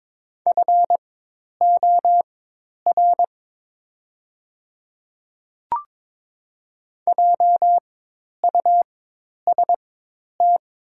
Morse code, heard as a keyed tone, about 22 wpm, spaced as for 10 wpm, 700 Hz. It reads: FOR JUST
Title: 4x Character Spacing - Top 200 Words - 22wpm